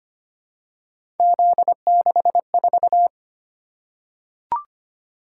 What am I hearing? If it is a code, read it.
Morse code, heard as a keyed tone, about 25 wpm, 700 Hz. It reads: Z64
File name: Three Letter-Number Combinations - 25wpm